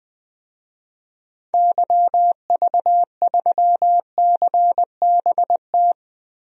Morse code, heard as a keyed tone, about 20 wpm, 700 Hz. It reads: YV3CBT
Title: DXCC-100 Call Signs - 6 Characters - Repeated 3x - 20wpm